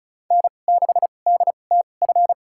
Morse code, heard as a keyed tone, about 36 wpm, spaced as for 25 wpm, 700 Hz. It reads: N6DTF
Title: USA Call Signs - 2x Character Spacing - 35wpm